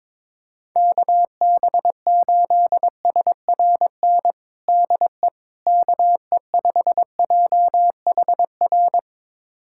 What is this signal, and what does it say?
Morse code, heard as a keyed tone, about 22 wpm, 700 Hz. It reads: KB8SRN DE KE5JHR
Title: USA Callsign DE Callsign - 6 Characters - 22wpm